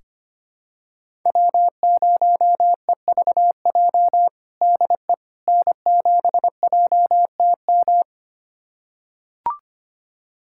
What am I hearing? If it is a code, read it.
Morse code, heard as a keyed tone, about 25 wpm, 700 Hz. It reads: W0EVJ DE N7JTM
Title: USA Callsign DE Callsign - 5 Characters - 25wpm